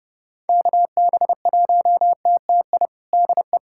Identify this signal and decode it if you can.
Morse code, heard as a keyed tone, about 32 wpm, 700 Hz. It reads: KB1TTI DE
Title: USA Callsign DE Callsign 30wpm